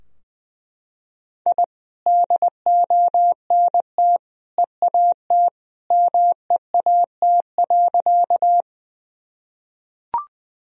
Morse code, heard as a keyed tone, about 20 wpm, 700 Hz. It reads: I DONT EAT MEAT.
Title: Sentences from Top 1000 Words - Review Part 3 - 20wpm